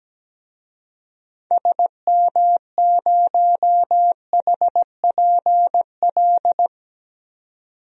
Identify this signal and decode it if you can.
Morse code, heard as a keyed tone, about 17 wpm, 700 Hz. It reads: SM0HPL
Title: DXCC-100 Call Signs - 6 Characters - 17wpm